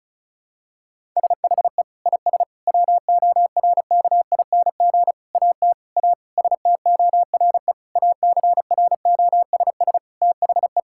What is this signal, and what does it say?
Morse code, heard as a keyed tone, about 34 wpm, 700 Hz. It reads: SHE IS WORKING AT A STORE ACROSS THE
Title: Sentences from Top 1000 Words - Review Part 5 - 35wpm